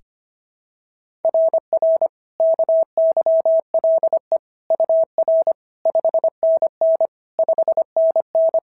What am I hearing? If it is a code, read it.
Morse code, heard as a keyed tone, about 25 wpm, 690 Hz. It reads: RR KYLE UR 5NN 5NN
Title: SOTA QSO Practice 25wpm